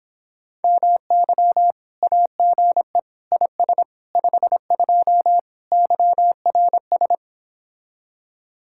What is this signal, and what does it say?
Morse code, heard as a keyed tone, about 29 wpm, 700 Hz. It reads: MY AGE IS 52 YRS